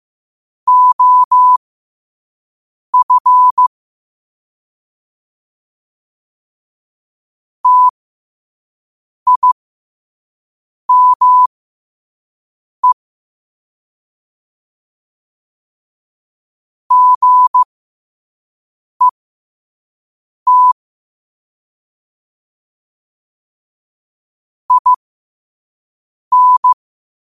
Morse code, heard as a keyed tone, about 15 wpm, spaced as for 5 wpm, 1000 Hz. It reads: OF TIME GET IN
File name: qotdfiles / 5wpm